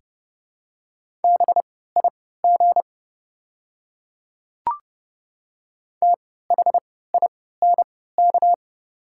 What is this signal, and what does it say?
Morse code, heard as a keyed tone, about 29 wpm, spaced as for 17 wpm, 700 Hz. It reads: BIG THINK
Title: 3x Character Spacing - Top 200 Words - 30wpm